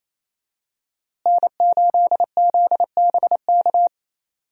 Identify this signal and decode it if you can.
Morse code, heard as a keyed tone, about 28 wpm, 700 Hz. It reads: N8ZBK
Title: USA Call Signs - 5 Characters - Repeated 3x - 28wpm